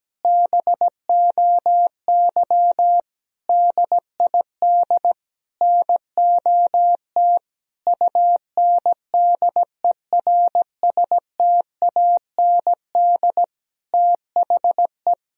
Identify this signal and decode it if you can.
Morse code, heard as a keyed tone, about 17 wpm, 700 Hz. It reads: BOY DID NOT UNDERSTAND THE
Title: Sentences from Top 1000 Words - Review Part 3 - 17wpm